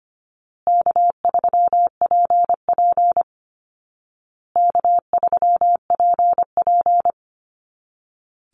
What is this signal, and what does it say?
Morse code, heard as a keyed tone, about 25 wpm, 700 Hz. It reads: K3PP K3PP
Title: Easy Call Signs Repeated 3x - 25wpm